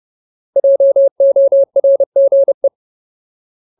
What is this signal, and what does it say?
Morse code, heard as a keyed tone, about 30 wpm, 550 Hz. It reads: JORGE
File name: Ultimate Word List / 30WPM Ultimate